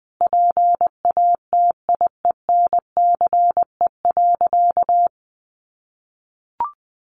Morse code, heard as a keyed tone, about 20 wpm, 700 Hz. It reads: PATIENCE.
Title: Sentences from Top 1300 Words 20wpm